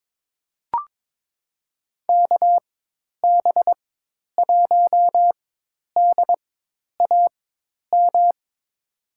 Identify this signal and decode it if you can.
Morse code, heard as a keyed tone, about 22 wpm, spaced as for 10 wpm, 700 Hz. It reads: KB1DAM
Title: USA Call Signs - 6 Characters - 4x Character Spacing - 22wpm